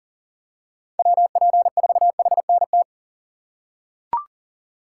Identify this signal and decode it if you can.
Morse code, heard as a keyed tone, about 40 wpm, 700 Hz. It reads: WP4HNT
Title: USA Call Signs - 6 Characters - 2x Character Spacing - 40wpm